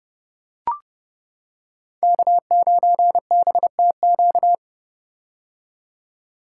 Morse code, heard as a keyed tone, about 30 wpm, 700 Hz. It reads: K9BTQ
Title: K1USN SST QSO Practice - 30wpm